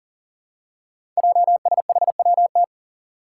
Morse code, heard as a keyed tone, about 38 wpm, 700 Hz. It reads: JSHWT
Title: Five Challenging Letter-Number Combinations 40wpm